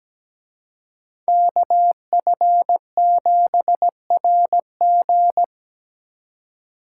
Morse code, heard as a keyed tone, about 20 wpm, 700 Hz. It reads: KF7RG
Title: Speed Racing - USA 2x2 Call Signs 17wpm